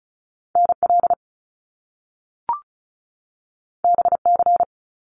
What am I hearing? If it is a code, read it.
Morse code, heard as a keyed tone, about 36 wpm, 700 Hz. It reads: NL BC
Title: Canadian Provinces and Territories 35wpm